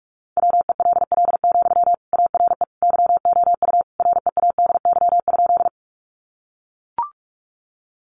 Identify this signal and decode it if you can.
Morse code, heard as a keyed tone, about 44 wpm, 700 Hz. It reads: WELL, ARE YOU READY?
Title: Sentences from Top 400 Words 45wpm